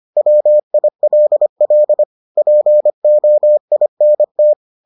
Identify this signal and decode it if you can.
Morse code, heard as a keyed tone, about 25 wpm, 600 Hz. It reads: WILL POINT